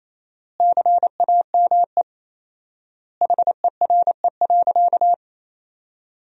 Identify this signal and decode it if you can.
Morse code, heard as a keyed tone, about 28 wpm, 700 Hz. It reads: CAME HERE.
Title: 4x Word-Spacing - Sentences from Top 500 Words - 28wpm